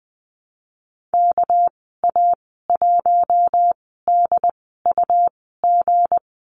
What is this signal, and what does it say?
Morse code, heard as a keyed tone, about 20 wpm, spaced as for 15 wpm, 700 Hz. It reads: KA1DUG
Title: 219 Lesson 35g - Number 0 - Call Signs 20wpm